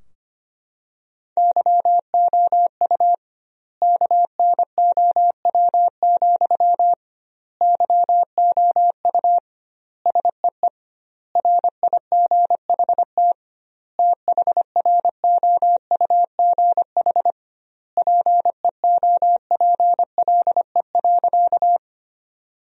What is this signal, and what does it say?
Morse code, heard as a keyed tone, about 25 wpm, 700 Hz. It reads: YOU KNOW, YOU SEE RIGHT THROUGH PEOPLE.
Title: 2x Word-Spacing - Sentences from Top 200 Words - 25wpm